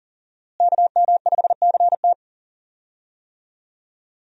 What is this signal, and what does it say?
Morse code, heard as a keyed tone, about 40 wpm, 700 Hz. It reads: KM5CT
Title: USA Call Signs - 5 Characters - Repeated 2x - 40wpm